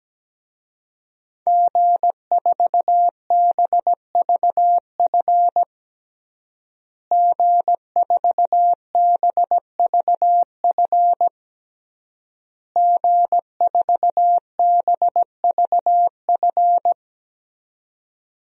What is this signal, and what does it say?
Morse code, heard as a keyed tone, about 17 wpm, 700 Hz. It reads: G4BVF G4BVF G4BVF
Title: DXCC-100 Call Signs - 5 Characters - Repeated 3x - 17wpm